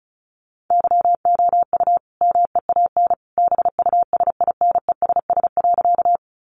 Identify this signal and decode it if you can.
Morse code, heard as a keyed tone, about 34 wpm, 700 Hz. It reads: YOU MEAN BUSINESS.
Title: Sentences from Top 600 Words 35wpm